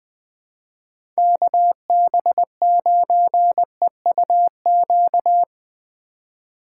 Morse code, heard as a keyed tone, about 21 wpm, 700 Hz. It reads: KB9EUQ
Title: USA Call Signs - 6 Characters - 4x Character Spacing - 20wpm